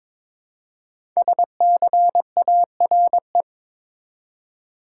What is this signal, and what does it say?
Morse code, heard as a keyed tone, about 22 wpm, 700 Hz. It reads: SCARE
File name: 5 Letter Words - First 3 Letters Given - 22wpm